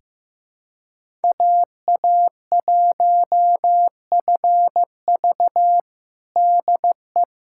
Morse code, heard as a keyed tone, about 15 wpm, 700 Hz. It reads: AA1FV DE
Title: USA Callsign DE Callsign - 5 Characters - 15wpm